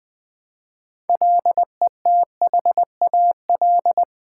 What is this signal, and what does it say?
Morse code, heard as a keyed tone, about 20 wpm, 700 Hz. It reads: LETHAL
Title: Build A Word - 2 Letters At A Time - 6-Letter Words - 20wpm